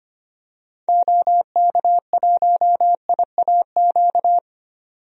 Morse code, heard as a keyed tone, about 25 wpm, 700 Hz. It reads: OK1IAQ
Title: DXCC-100 Call Signs - 6 Characters - 25wpm